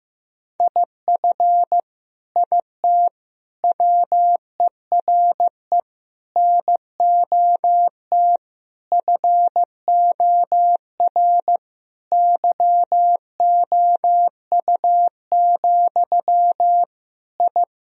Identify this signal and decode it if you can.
Morse code, heard as a keyed tone, about 15 wpm, 700 Hz. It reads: IF IT WERE NOT FOR YOU, I
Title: Sentences from Top 1000 Words - Review Part 2 - 15wpm